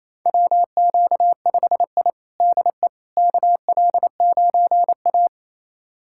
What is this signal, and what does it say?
Morse code, heard as a keyed tone, about 28 wpm, 700 Hz. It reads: WQ5I DE KL9A
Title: USA Callsign DE Callsign - 4 Characters - 28wpm